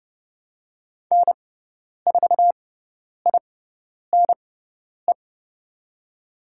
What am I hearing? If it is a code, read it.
Morse code, heard as a keyed tone, about 30 wpm, spaced as for 10 wpm, 700 Hz. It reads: N4INE
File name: Lesson 17g - Letter u - Call Signs 30wpm